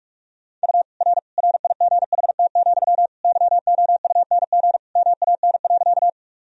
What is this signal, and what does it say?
Morse code, heard as a keyed tone, about 47 wpm, 700 Hz. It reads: U R RIGHT, YOUNG MAN.
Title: Sentences from Top 400 Words and Abbreviations 45wpm